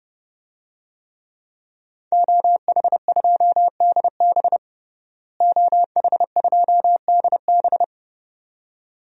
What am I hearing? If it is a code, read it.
Morse code, heard as a keyed tone, about 30 wpm, 700 Hz. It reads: OH2DB OH2DB
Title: DXCC-100 Call Signs - 5 Characters - Repeated 2x - 30wpm